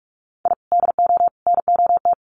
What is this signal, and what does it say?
Morse code, heard as a keyed tone, about 44 wpm, 700 Hz. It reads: I DO NOT
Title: Sets of 5 Words 45wpm